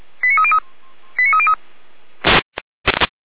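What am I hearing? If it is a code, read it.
Five-tone selective call (ZVEI), heard as quick runs of five tones, tones of about 70 ms. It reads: 89303 89303
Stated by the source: ZVEI five-tone selective call 89286, received on an SDR.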